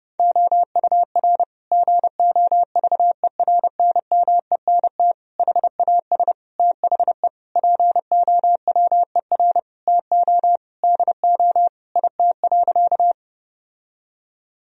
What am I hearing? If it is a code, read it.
Morse code, heard as a keyed tone, about 32 wpm, 700 Hz. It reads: OUR GOVERNMENT HAS THE POWER TO DO IT.
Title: Sentences from Top 1000 Words - Review Part 3 - 30wpm